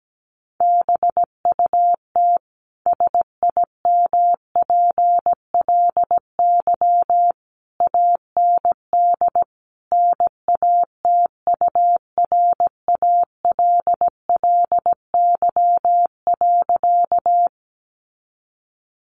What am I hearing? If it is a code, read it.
Morse code, heard as a keyed tone, about 17 wpm, 700 Hz. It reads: BUT SIMPLY AND NATURALLY.
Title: Sentences from Top 1300 Words 17wpm